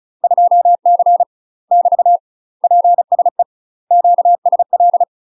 Morse code, heard as a keyed tone, about 36 wpm, 700 Hz. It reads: 2C <BT> PSE QSL
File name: UK MASTER QSOs TRAINING / 35WPM / UK MASTER QSOs 0009